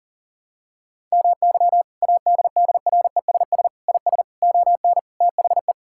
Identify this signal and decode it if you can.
Morse code, heard as a keyed tone, about 37 wpm, 700 Hz. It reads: MY ADDRESS IS ON THE
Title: Sentences from Top 1000 Words - Review Part 6 - 40wpm